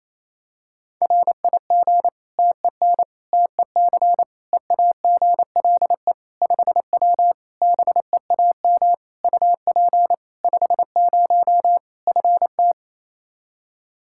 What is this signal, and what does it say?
Morse code, heard as a keyed tone, about 28 wpm, 700 Hz. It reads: RIG TEN TEC EAGLE 5W BEAM UP 50 FT
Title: Practice QSO - Exchange 2 - Info - 28wpm